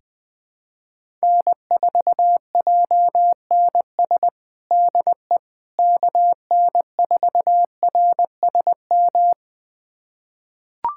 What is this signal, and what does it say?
Morse code, heard as a keyed tone, about 19 wpm, 700 Hz. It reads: N4JNS DE KN4RSM
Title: USA Callsign DE Callsign 20wpm